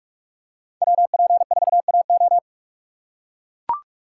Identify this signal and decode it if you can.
Morse code, heard as a keyed tone, about 44 wpm, 700 Hz. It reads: WP4AO
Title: USA Call Signs - 5 Characters - 4x Character Spacing - 45wpm